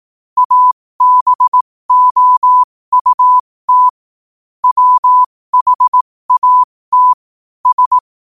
Morse code, heard as a keyed tone, about 18 wpm, spaced as for 15 wpm, 1000 Hz. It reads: ABOUT WHAT S
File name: qotdfiles / 15wpm